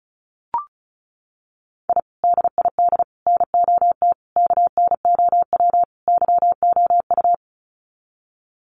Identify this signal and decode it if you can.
Morse code, heard as a keyed tone, about 34 wpm, 700 Hz. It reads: I DID NOT KNOW YOU
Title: Sets of 5 Words Encore2 35wpm